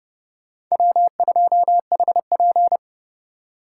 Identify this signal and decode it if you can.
Morse code, heard as a keyed tone, about 36 wpm, 700 Hz. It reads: W2HP
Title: Speed Racing - USA Call Signs 30wpm